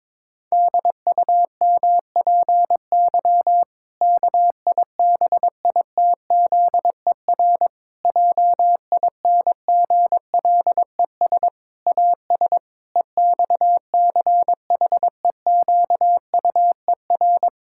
Morse code, heard as a keyed tone, about 21 wpm, 700 Hz. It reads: DUMPY KIBITZER JINGLES AS EXCHEQUER
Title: Pangram Challenge v2 - 22wpm